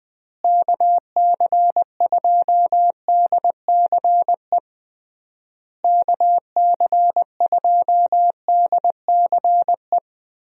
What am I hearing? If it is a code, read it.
Morse code, heard as a keyed tone, about 21 wpm, 700 Hz. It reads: KC2DCE KC2DCE
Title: USA Call Signs - 6 Characters - Repeated 3x - 20wpm